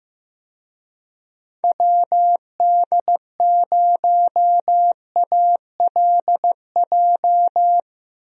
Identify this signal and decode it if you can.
Morse code, heard as a keyed tone, about 15 wpm, 700 Hz. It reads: WD0ALJ
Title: USA Call Signs - 6 Characters - Repeated 3x - 15wpm